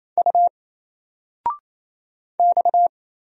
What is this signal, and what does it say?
Morse code, heard as a keyed tone, about 27 wpm, 700 Hz. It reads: U X
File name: Single-Letter-Number-28wpm